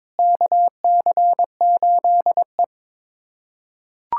Morse code, heard as a keyed tone, about 21 wpm, 700 Hz. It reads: KC8E
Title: USA Callsign DE Callsign - 4 Characters - 22wpm